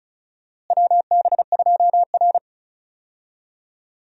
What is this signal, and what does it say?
Morse code, heard as a keyed tone, about 34 wpm, 700 Hz. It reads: WB2R
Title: USA Call Signs - 4 Characters - Repeated 2x - 35wpm